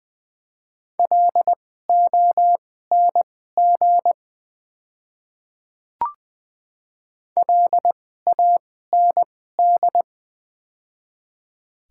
Morse code, heard as a keyed tone, about 20 wpm, spaced as for 15 wpm, 700 Hz. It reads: LONG LAND
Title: 2x Character Spacing - Top 200 Words - 20wpm